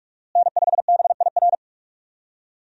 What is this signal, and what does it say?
Morse code, heard as a keyed tone, about 46 wpm, 700 Hz. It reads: N5BIR
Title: USA Call Signs - 5 Characters - Repeated 3x - 45wpm